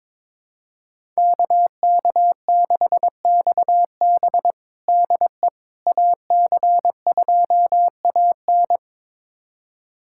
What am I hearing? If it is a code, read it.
Morse code, heard as a keyed tone, about 22 wpm, 700 Hz. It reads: KK6XB DE AC2AN
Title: USA Callsign DE Callsign - 5 Characters - 22wpm